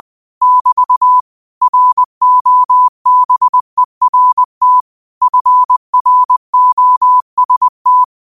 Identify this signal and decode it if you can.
Morse code, heard as a keyed tone, about 20 wpm, 1000 Hz. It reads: <BT> ROBERT FROST
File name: qotdfiles / 20wpm